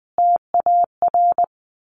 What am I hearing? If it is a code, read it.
Morse code, heard as a keyed tone, about 20 wpm, 700 Hz. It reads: TAR
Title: A Sound Of Thunder 20wpm